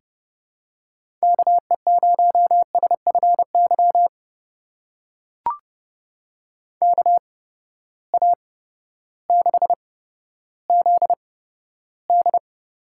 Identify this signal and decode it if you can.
Morse code, heard as a keyed tone, about 30 wpm, spaced as for 14 wpm, 700 Hz. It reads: KE0SFY KA6ZD
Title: USA Call Signs - 6 Characters - 8x Character Spacing - 30wpm